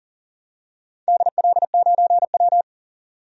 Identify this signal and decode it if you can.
Morse code, heard as a keyed tone, about 40 wpm, 700 Hz. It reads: DL9W
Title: DXCC-100 Call Signs - 4 Characters - 40wpm